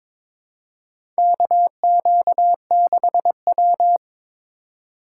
Morse code, heard as a keyed tone, about 22 wpm, 700 Hz. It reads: KQ6W
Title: USA Call Signs - 2x Character Spacing - 22wpm